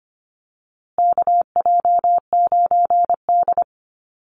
Morse code, heard as a keyed tone, about 25 wpm, 700 Hz. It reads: KJ9D
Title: USA Call Signs Repeated 3x 25wpm